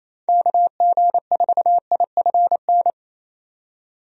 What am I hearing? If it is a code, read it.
Morse code, heard as a keyed tone, about 28 wpm, 700 Hz. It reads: KG4IFN
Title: USA Call Signs - 6 Characters - 28wpm